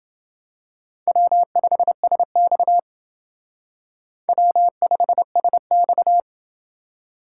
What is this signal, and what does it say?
Morse code, heard as a keyed tone, about 29 wpm, 700 Hz. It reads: W5SX W5SX